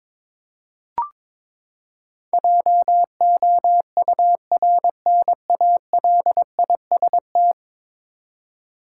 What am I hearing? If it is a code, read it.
Morse code, heard as a keyed tone, about 22 wpm, 700 Hz. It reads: JOURNALIST
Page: morse-code-ninja-practice-22wpm